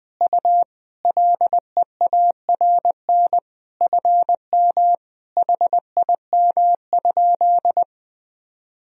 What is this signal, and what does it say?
Morse code, heard as a keyed tone, about 20 wpm, 700 Hz. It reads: U LEARN FM HIM?
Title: Sentences from Top 300 Words and Abbreviations 20wpm